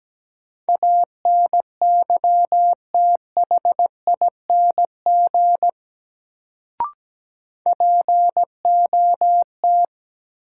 Morse code, heard as a keyed tone, about 17 wpm, 700 Hz. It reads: ANYTHING POT
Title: Sentences from Top 1000 Words - New Word Drill - 17wpm